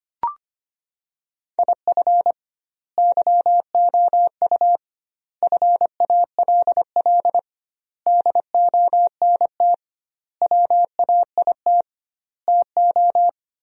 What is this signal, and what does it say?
Morse code, heard as a keyed tone, about 25 wpm, 700 Hz. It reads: IF YOU FALL DONT WAIT TO
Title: 2x Word-Spacing - Sentences from Top 500 Words - 25wpm